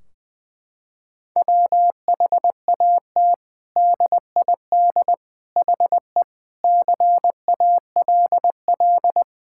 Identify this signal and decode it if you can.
Morse code, heard as a keyed tone, about 20 wpm, 700 Hz. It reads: WHAT DID HE CALL
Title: Sentences from Top 1600 Words 20wpm